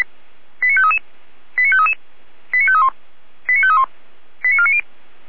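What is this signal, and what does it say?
Five-tone selective call (ZVEI), heard as quick runs of five tones, tones of about 70 ms. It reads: 89533 89533 89531 89531 89490
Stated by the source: ZVEI five-tone call 89485 received on an SDR.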